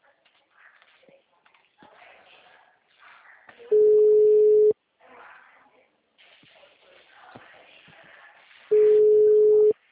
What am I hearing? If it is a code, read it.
Ringback tone (Europe, 425 Hz).
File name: out- -2023